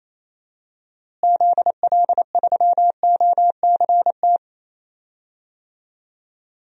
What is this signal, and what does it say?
Morse code, heard as a keyed tone, about 28 wpm, 700 Hz. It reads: ZL3OCT
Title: DXCC-100 Call Signs - 6 Characters - Repeated 3x - 28wpm